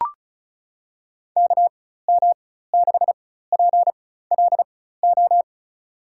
Morse code, heard as a keyed tone, about 36 wpm, spaced as for 16 wpm, 700 Hz. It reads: KM6PLO
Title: USA Call Signs - 4x Character Spacing - 35wpm